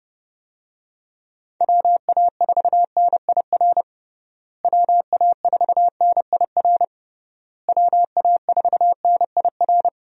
Morse code, heard as a keyed tone, about 29 wpm, 700 Hz. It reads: WA4NIR WA4NIR WA4NIR
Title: USA Call Signs - 6 Characters - Repeated 3x - 30wpm